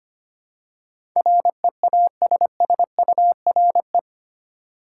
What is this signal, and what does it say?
Morse code, heard as a keyed tone, about 25 wpm, 700 Hz. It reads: REASSURE
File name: Eight Letter Words - 25wpm